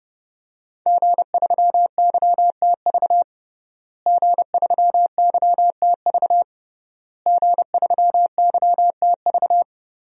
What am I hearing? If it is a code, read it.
Morse code, heard as a keyed tone, about 30 wpm, 700 Hz. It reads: G3YTV G3YTV G3YTV
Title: DXCC-100 Call Signs - 5 Characters - Repeated 3x - 30wpm